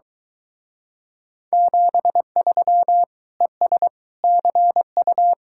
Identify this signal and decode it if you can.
Morse code, heard as a keyed tone, about 23 wpm, 700 Hz. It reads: 73 ES CU